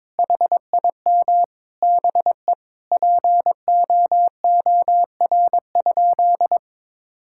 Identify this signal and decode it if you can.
Morse code, heard as a keyed tone, about 22 wpm, 700 Hz. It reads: HIM BE POOR?
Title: Sentences from Top 1600 Words 22wpm